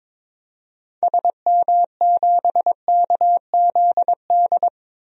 Morse code, heard as a keyed tone, about 22 wpm, 700 Hz. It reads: SM7KZD
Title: DXCC-100 Call Signs - 6 Characters - 22wpm